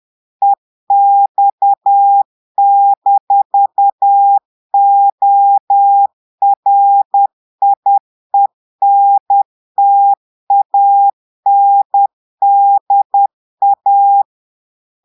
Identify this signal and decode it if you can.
Morse code, heard as a keyed tone, about 10 wpm, 800 Hz. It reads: EX-ORIENTANDA
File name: Portugais - texte en clair 1 - vitesse - 10 mots minute